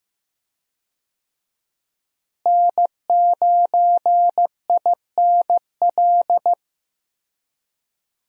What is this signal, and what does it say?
Morse code, heard as a keyed tone, about 15 wpm, 700 Hz. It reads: N9INL
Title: USA Call Signs - 5 Characters - Repeated 3x - 15wpm